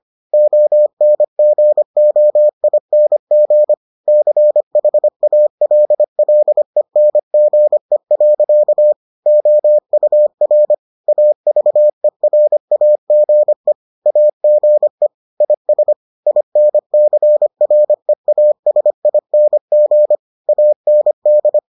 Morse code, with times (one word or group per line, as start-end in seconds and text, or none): 0.3-3.7 ONGOING
4.1-8.9 CHALLENGE.
9.3-10.7 OUR
11.1-13.7 AVERAGE
14.1-15.1 AGE
15.4-15.9 IS
16.3-20.2 INCREASING
20.5-21.6 AND